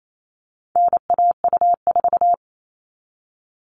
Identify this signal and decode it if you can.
Morse code, heard as a keyed tone, about 28 wpm, 700 Hz. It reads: NAU4
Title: All Cut Numbers - 4-Digit 28wpm